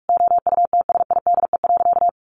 Morse code, heard as a keyed tone, about 44 wpm, 700 Hz. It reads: OUTSIDE.
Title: Sentences from Top 1200 Words 45wpm